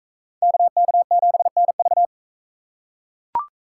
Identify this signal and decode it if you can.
Morse code, heard as a keyed tone, about 38 wpm, 700 Hz. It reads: KK7NV